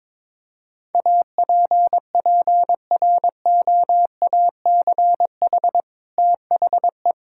Morse code, heard as a keyed tone, about 21 wpm, 700 Hz. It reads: APPROACH THE
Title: Sentences from Top 1000 Words - Review Part 6 - 22wpm